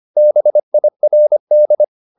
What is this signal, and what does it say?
Morse code, heard as a keyed tone, about 25 wpm, 600 Hz. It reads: BIRD
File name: the old man and the sea part 1 25wpm